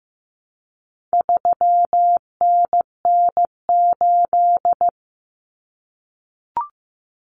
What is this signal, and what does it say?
Morse code, heard as a keyed tone, about 15 wpm, 700 Hz. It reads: 3NN8
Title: Common Cut Numbers - 4-Digit 15wpm